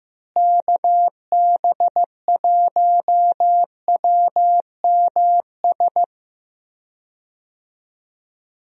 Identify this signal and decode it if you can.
Morse code, heard as a keyed tone, about 15 wpm, 700 Hz. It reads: KB1WMS
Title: USA Call Signs - 6 Characters - Repeated 2x - 15wpm